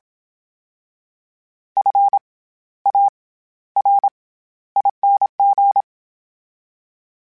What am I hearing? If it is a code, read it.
Morse code, heard as a keyed tone, about 27 wpm, 800 Hz. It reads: F A R ING
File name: Special INGs-mp3s / Special INGs-mp3s / SlowToFaster ing @25